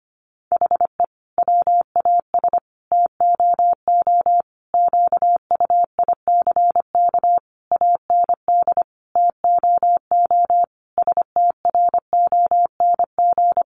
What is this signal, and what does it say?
Morse code, heard as a keyed tone, about 25 wpm, 700 Hz. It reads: HE WAS TOO QUICK AND TOO STRONG
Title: Sentences from Top 600 Words 25wpm